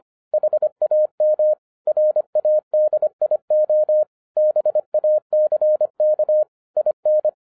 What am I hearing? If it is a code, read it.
Morse code, heard as a keyed tone, about 26 wpm, 610 Hz. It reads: HAM RADIO BACK IN
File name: 2011-10-mp3 / n04-JUST MOVED 25wpm